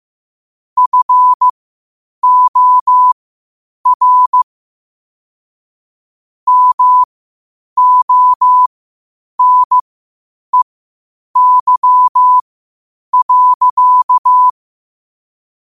Morse code, heard as a keyed tone, about 15 wpm, spaced as for 8 wpm, 1000 Hz. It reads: FOR MONEY.
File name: qotdfiles / 8wpm